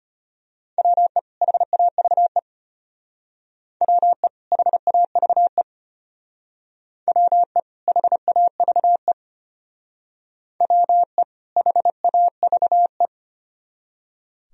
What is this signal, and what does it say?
Morse code, heard as a keyed tone, about 32 wpm, 700 Hz. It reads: WE HAVE WE HAVE WE HAVE WE HAVE